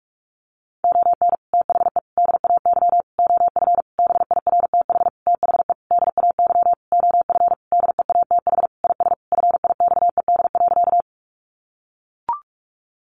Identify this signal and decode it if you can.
Morse code, heard as a keyed tone, about 44 wpm, 700 Hz. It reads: ON THE DAY OF BIRTH THE DAY OF DEATH IS FIXED.
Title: Sentences from Top 1100 Words 45wpm